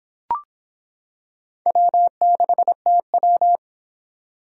Morse code, heard as a keyed tone, about 26 wpm, 700 Hz. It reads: W6TW